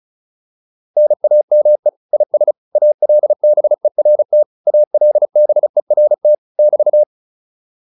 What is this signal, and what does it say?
Morse code, heard as a keyed tone, about 35 wpm, 600 Hz. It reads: NAME IS ALBERT ALBERT <BT>